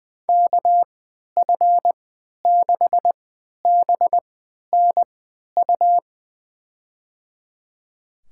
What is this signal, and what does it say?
Morse code, heard as a keyed tone, about 20 wpm, spaced as for 11 wpm, 700 Hz. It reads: KF6BNU
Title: USA Call Signs - 6 Characters - 3x Character Spacing - 20wpm